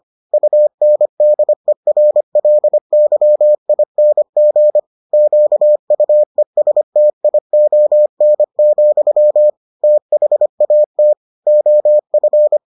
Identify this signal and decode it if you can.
Morse code, heard as a keyed tone, about 25 wpm, 600 Hz. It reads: UNDERLYING QUESTION, THAT OF